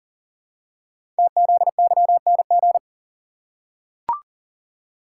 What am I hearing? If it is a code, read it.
Morse code, heard as a keyed tone, about 40 wpm, 690 Hz. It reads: TZYNG
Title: Code Groups - 5 Letter:Numbers - 40wpm